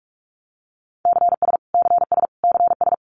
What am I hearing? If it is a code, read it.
Morse code, heard as a keyed tone, about 44 wpm, 700 Hz. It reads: CS CS CS
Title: Abbreviations Repeated 3x - 45wpm